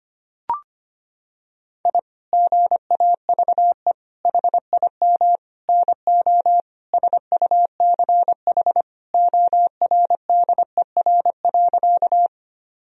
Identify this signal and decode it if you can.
Morse code, heard as a keyed tone, about 25 wpm, 700 Hz. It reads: I GAVE HIM NO SUCH ORDER.
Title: Sentences from Top 1000 Words - Review Part 2 - 25wpm